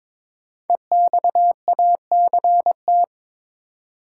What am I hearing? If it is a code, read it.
Morse code, heard as a keyed tone, about 22 wpm, 700 Hz. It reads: EXACT
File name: 5 Letter Words - First 3 Letters Given - 22wpm